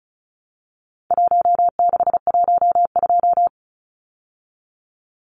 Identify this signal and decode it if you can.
Morse code, heard as a keyed tone, about 34 wpm, 700 Hz. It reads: 1612
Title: Numbers - Contest Style - 4-Digit 35wpm